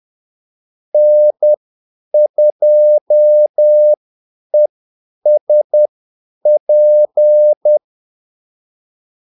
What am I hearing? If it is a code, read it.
Morse code, heard as a keyed tone, about 10 wpm, spaced as for 8 wpm, 600 Hz. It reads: N2ESP
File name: Callsigns-10-8wpm